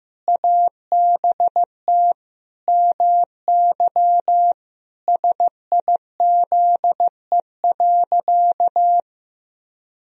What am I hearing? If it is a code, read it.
Morse code, heard as a keyed tone, about 15 wpm, 700 Hz. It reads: ABT MY SIZE.
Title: Sentences from Top 400 Words and Abbreviations 15wpm